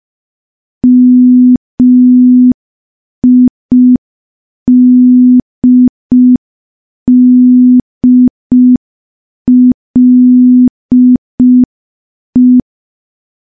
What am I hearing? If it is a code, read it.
Morse code, heard as a keyed tone, about 5 wpm, 250 Hz. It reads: MIDDLE
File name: morse / morse-training-05WPM